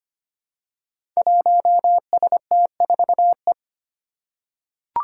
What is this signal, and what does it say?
Morse code, heard as a keyed tone, about 25 wpm, 700 Hz. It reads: 1ST4E
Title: Five Challenging Letter-Number Combinations 25wpm